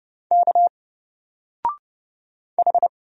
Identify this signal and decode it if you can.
Morse code, heard as a keyed tone, about 29 wpm, 700 Hz. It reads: K H
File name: Single-Letter-Number-30wpm